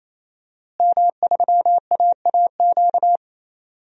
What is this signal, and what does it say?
Morse code, heard as a keyed tone, about 28 wpm, 700 Hz. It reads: M3AAQ
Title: DXCC-100 Call Signs - 5 Characters - Repeated 3x - 28wpm